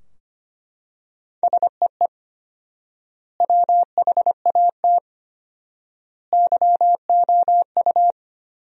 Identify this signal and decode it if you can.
Morse code, heard as a keyed tone, about 25 wpm, 700 Hz. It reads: SEE WHAT YOU
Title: 4x Word-Spacing - Sentences from Top 500 Words - 25wpm